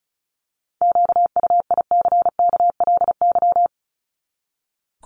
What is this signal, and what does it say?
Morse code, heard as a keyed tone, about 34 wpm, 700 Hz. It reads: QUICKLY
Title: LY Suffix - Top 100 Words with Base Words 35wpm